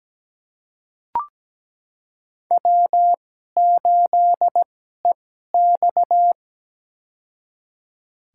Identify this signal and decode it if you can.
Morse code, heard as a keyed tone, about 17 wpm, spaced as for 12 wpm, 700 Hz. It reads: W8EX
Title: USA Call Signs - 4 Characters - 2x Character Spacing - 17wpm